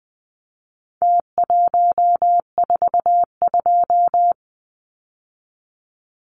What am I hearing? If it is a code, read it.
Morse code, heard as a keyed tone, about 20 wpm, 700 Hz. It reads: T142
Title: Numbers - Contest Style - 4-Digit 20wpm